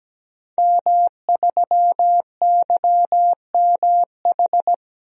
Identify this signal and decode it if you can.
Morse code, heard as a keyed tone, about 17 wpm, 700 Hz. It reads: M3YMH
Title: DXCC-100 Call Signs - 5 Characters - Repeated 2x - 17wpm